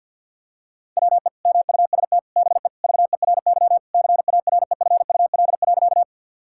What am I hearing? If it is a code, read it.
Morse code, heard as a keyed tone, about 51 wpm, 700 Hz. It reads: WE MUST BE VERY CAREFUL.
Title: Sentences from Top 1600 Words 50wpm